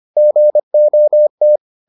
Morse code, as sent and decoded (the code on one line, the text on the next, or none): --. --- -
GOT